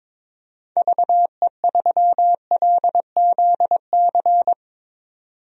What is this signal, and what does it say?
Morse code, heard as a keyed tone, about 22 wpm, 700 Hz. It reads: VE3LZC
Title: DXCC-100 Call Signs - 6 Characters - 22wpm